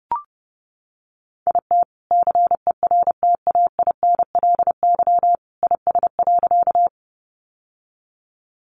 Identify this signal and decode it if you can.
Morse code, heard as a keyed tone, about 30 wpm, 700 Hz. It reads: IT CERTAINLY IS.
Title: Sentences from Top 900 Words 30wpm